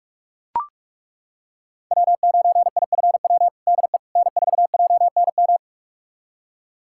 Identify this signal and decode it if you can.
Morse code, heard as a keyed tone, about 41 wpm, 700 Hz. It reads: W9IFW DE N4JNM
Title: USA Callsign DE Callsign - 5 Characters - 45wpm